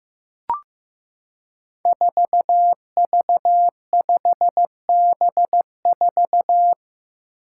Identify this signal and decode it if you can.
Morse code, heard as a keyed tone, about 15 wpm, 700 Hz. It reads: 4V5B4
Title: Five Challenging Letter-Number Combinations 15wpm